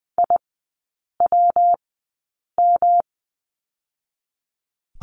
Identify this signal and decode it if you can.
Morse code, heard as a keyed tone, about 20 wpm, spaced as for 8 wpm, 690 Hz. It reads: IWM
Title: 145 Lesson 24e - Letter y - Three Characters 20wpm